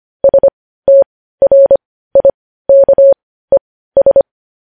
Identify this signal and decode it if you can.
Morse code, heard as a keyed tone, about 25 wpm, spaced as for 15 wpm, 550 Hz. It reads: STRIKES
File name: podcast / News headlines-1x-25x15